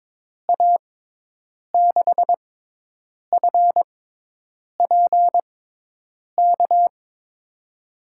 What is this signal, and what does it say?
Morse code, heard as a keyed tone, about 21 wpm, spaced as for 8 wpm, 700 Hz. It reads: A6FPK
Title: USA Call Signs - 6 Characters - 6x Character Spacing - 22wpm